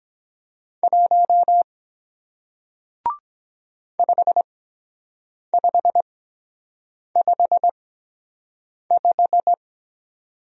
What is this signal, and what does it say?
Morse code, heard as a keyed tone, about 23 wpm, spaced as for 7 wpm, 700 Hz. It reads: 1 5555